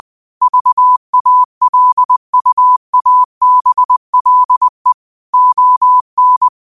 Morse code, heard as a keyed tone, about 20 wpm, 1000 Hz. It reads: VALUABLE ON
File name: qotdfiles / 20wpm